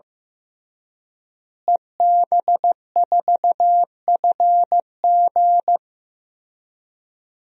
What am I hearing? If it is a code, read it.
Morse code, heard as a keyed tone, about 16 wpm, 700 Hz. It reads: EB4FG
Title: DXCC-100 Call Signs - 5 Characters - Repeated 3x - 15wpm